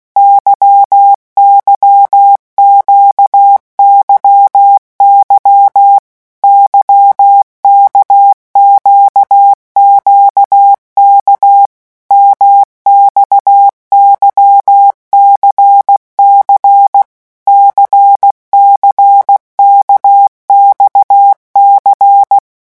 Morse code, heard as a keyed tone, about 16 wpm, 800 Hz. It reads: YYQYY YKQQK MXYCC CCKXC
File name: morse mp3 / revision1-vitesse 15 mots